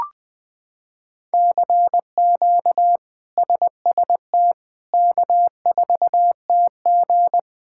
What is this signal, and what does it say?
Morse code, heard as a keyed tone, about 20 wpm, 700 Hz. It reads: CQ SST K4TG
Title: K1USN SST QSO Practice - 20wpm